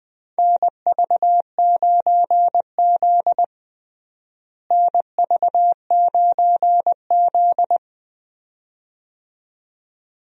Morse code, heard as a keyed tone, about 20 wpm, 700 Hz. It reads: NV9Z NV9Z
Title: USA Call Signs - 4 Characters - Repeated 2x - 20wpm